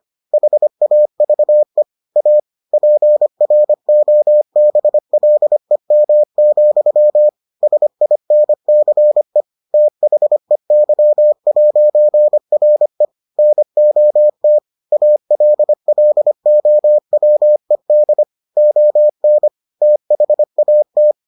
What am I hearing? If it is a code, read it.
Morse code, heard as a keyed tone, about 25 wpm, 600 Hz. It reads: HAVE A PROBLEM, SINCE THEY'RE NOT ALLOWED ON THAT